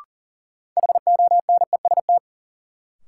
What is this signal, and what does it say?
Morse code, heard as a keyed tone, about 40 wpm, 700 Hz. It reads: HONEST
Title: Sets of 5 Words Encore - New Word Drill - 40wpm